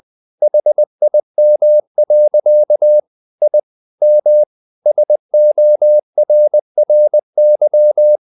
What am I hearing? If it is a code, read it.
Morse code, heard as a keyed tone, about 20 wpm, 600 Hz. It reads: HIM. I M SORRY
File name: the old man and the sea part 2 20wpm